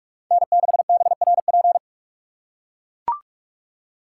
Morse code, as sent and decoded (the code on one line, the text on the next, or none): -. -.... -... .-. .--.
N6BRP